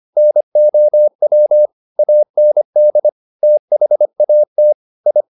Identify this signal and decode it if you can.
Morse code, heard as a keyed tone, about 25 wpm, 600 Hz. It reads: NOW AND THAT I